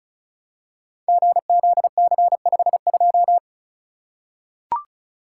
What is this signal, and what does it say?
Morse code, heard as a keyed tone, about 36 wpm, 700 Hz. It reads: GZC52
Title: Code Groups - 5 Letter:Numbers - 35wpm